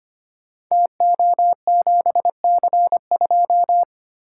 Morse code, heard as a keyed tone, about 25 wpm, 700 Hz. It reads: TO7C2
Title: Code Groups - 5 Letter:Numbers - 25wpm